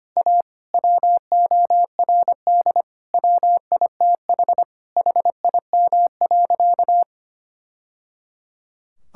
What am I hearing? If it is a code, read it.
Morse code, heard as a keyed tone, about 25 wpm, 700 Hz. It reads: A WORD WITH HIM.
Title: Sentences from Top 1000 Words - Review Part 6 - 25wpm